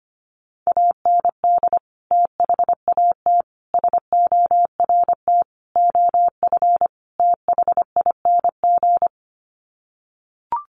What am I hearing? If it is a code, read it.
Morse code, heard as a keyed tone, about 25 wpm, 700 Hz. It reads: AND THAT SORT OF THING
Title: Sets of 5 Words Encore2 25wpm